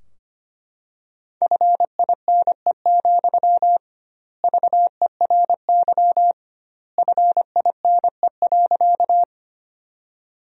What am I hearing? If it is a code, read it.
Morse code, heard as a keyed tone, about 25 wpm, 700 Hz. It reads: FINE, VERY FINE.
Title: 2x Word-Spacing - Sentences from Top 500 Words - 25wpm